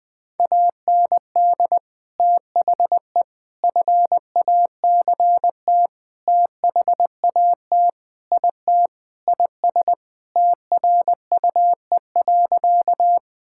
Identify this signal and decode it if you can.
Morse code, heard as a keyed tone, about 20 wpm, 700 Hz. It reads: AND THE FACT THAT IT IS TRUE.
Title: Sentences with 4-Word Phrases - 20wpm